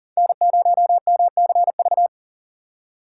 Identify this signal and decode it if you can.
Morse code, heard as a keyed tone, about 39 wpm, 700 Hz. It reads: N0MCV
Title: USA Call Signs - 5 Characters - 3x Character Spacing - 40wpm